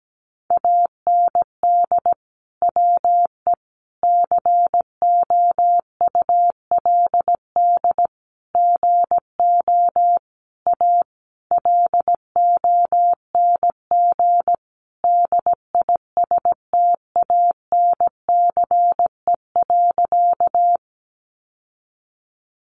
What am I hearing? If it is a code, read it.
Morse code, heard as a keyed tone, about 17 wpm, 700 Hz. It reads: AND WE COULD GO A LONG DISTANCE.
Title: Sentences from Top 1200 Words 17wpm